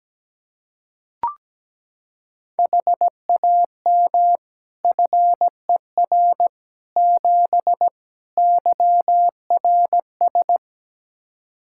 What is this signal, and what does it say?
Morse code, heard as a keyed tone, about 17 wpm, 700 Hz. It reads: HAM FER 7 YRS
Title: Practice QSO - Exchange 3 - More Info - 17wpm